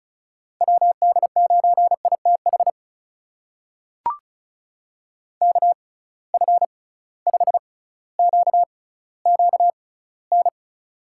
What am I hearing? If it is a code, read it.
Morse code, heard as a keyed tone, about 35 wpm, spaced as for 18 wpm, 700 Hz. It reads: WD9ITH KF5QQN
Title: USA Call Signs - 6 Characters - 6x Character Spacing - 35wpm